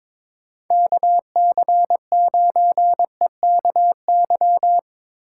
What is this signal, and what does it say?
Morse code, heard as a keyed tone, about 22 wpm, 700 Hz. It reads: KC9EKY
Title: USA Call Signs - 6 Characters - 2x Character Spacing - 22wpm